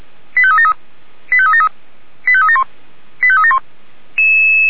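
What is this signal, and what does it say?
Five-tone selective call (ZVEI), heard as quick runs of five tones, tones of about 70 ms. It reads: 85373 85373 85371 85371
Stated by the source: ZVEI five-tone call received by SDR.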